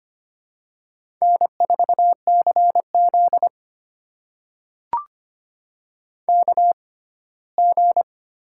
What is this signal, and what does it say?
Morse code, heard as a keyed tone, about 26 wpm, spaced as for 17 wpm, 700 Hz. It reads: N4CZ KG
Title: USA Call Signs - 6x Character Spacing - 25wpm